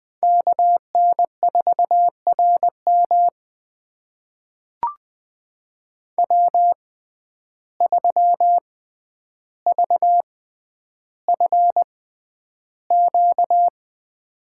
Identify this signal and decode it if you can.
Morse code, heard as a keyed tone, about 20 wpm, spaced as for 10 wpm, 700 Hz. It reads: KN4RM W3VFQ
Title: USA Call Signs - 5 Characters - 6x Character Spacing - 20wpm